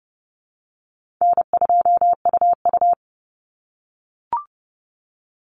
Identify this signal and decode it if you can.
Morse code, heard as a keyed tone, about 30 wpm, 700 Hz. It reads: N2UU
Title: Call Signs - 30wpm